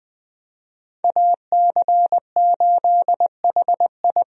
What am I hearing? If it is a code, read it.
Morse code, heard as a keyed tone, about 20 wpm, 700 Hz. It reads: AC8HI
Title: USA Call Signs - 5 Characters - 20wpm